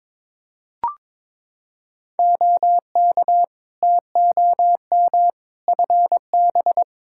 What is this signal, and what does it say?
Morse code, heard as a keyed tone, about 22 wpm, 700 Hz. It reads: OK TOM FB
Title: Practice QSO - Exchange 2 - Info - 22wpm